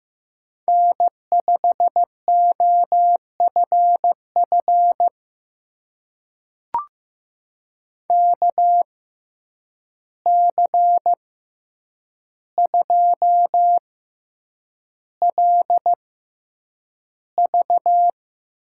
Morse code, heard as a keyed tone, about 15 wpm, spaced as for 8 wpm, 700 Hz. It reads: N5OFF KC2LV
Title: USA Call Signs - 6x Character Spacing - 15wpm